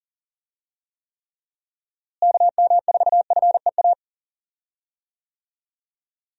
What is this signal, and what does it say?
Morse code, heard as a keyed tone, about 40 wpm, 700 Hz. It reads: KM4FEA
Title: USA Call Signs - 6 Characters - Repeated 3x - 40wpm